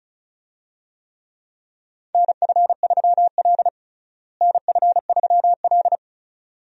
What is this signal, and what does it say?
Morse code, heard as a keyed tone, about 34 wpm, 700 Hz. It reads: NF3L NF3L
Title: USA Call Signs - 4 Characters - Repeated 2x - 35wpm